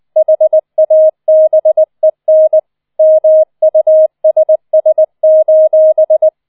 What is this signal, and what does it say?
Morse code, heard as a keyed tone, about 20 wpm, 630 Hz. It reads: HABEN MUSS: